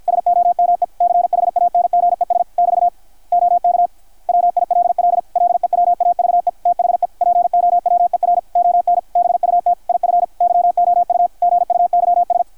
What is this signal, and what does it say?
Morse code, heard as a keyed tone, about 53 wpm, 660 Hz, with radio noise. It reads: SOME CHANGES <BT> OK WILL LEWAVE THE POWER ON BUT IF YOU GUYS
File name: archived / CRSnet